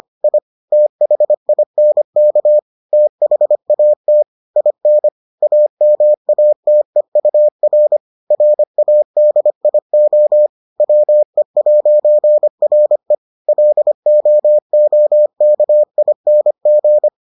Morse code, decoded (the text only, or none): I THINK THAT IN AMATEUR RADIO WE'RE LOOKING